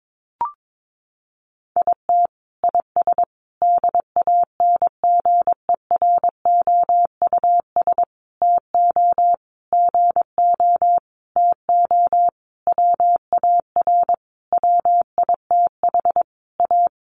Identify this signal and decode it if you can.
Morse code, heard as a keyed tone, about 22 wpm, 700 Hz. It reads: IT IS DANGEROUS TO GO TO WAR WITH A
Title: Sentences from Top 1200 Words 22wpm